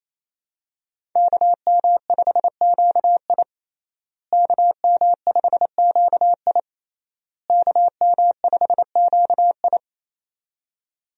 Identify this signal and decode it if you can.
Morse code, heard as a keyed tone, about 28 wpm, 700 Hz. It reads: KM5QI KM5QI KM5QI
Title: USA Call Signs - 5 Characters - Repeated 3x - 28wpm